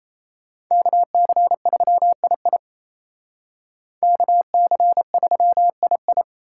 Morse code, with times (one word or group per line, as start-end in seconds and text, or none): 0.7-2.6 KC3II
4.0-6.2 KC3II